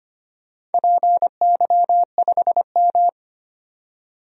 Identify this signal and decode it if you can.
Morse code, heard as a keyed tone, about 25 wpm, 700 Hz. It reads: PY5M
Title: DXCC-100 Call Signs - 4 Characters - 25wpm